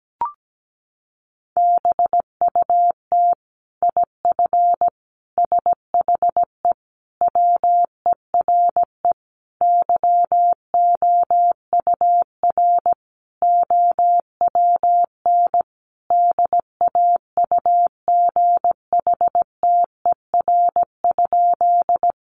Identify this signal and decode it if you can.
Morse code, heard as a keyed tone, about 17 wpm, 700 Hz. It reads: BUT IF SHE WERE YOUR OWN DAUGHTER?
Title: Sentences from Top 1500 Words 17wpm